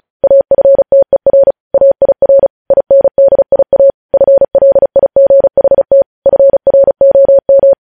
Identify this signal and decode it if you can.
Morse code, heard as a keyed tone, about 36 wpm, 550 Hz. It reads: AFTER AIR INDIA FLIGHT FROM
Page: news headlines in morse code at 35 wpm